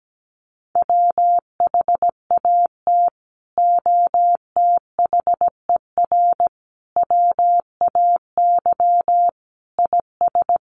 Morse code, heard as a keyed tone, about 17 wpm, 700 Hz. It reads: WHAT OTHER WAY IS
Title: Sentences from Top 100 Words 17wpm